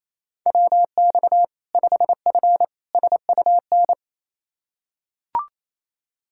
Morse code, heard as a keyed tone, about 28 wpm, 700 Hz. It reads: WX 5F SUN
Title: Practice QSO - Exchange 2 - Info - 28wpm